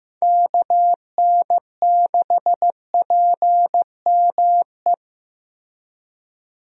Morse code, heard as a keyed tone, about 15 wpm, 700 Hz. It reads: KN6PME
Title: USA Call Signs - 6 Characters - Repeated 2x - 15wpm